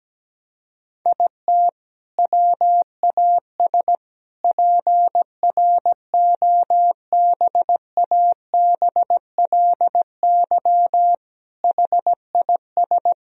Morse code, heard as a keyed tone, about 17 wpm, 700 Hz. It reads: IT WAS PROBABLY HIS
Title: Sentences from Top 1000 Words - Review Part 4 - 17wpm